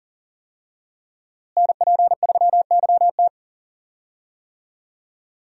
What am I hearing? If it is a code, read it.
Morse code, heard as a keyed tone, about 40 wpm, 700 Hz. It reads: NP3YT
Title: USA Call Signs - 5 Characters - Repeated 2x - 40wpm